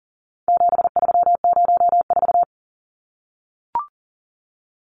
Morse code, heard as a keyed tone, about 40 wpm, 700 Hz. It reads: 7304
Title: Numbers - 4-Digit 40wpm